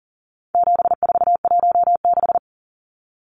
Morse code, heard as a keyed tone, about 40 wpm, 700 Hz. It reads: 7416
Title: Numbers - 4-Digit 40wpm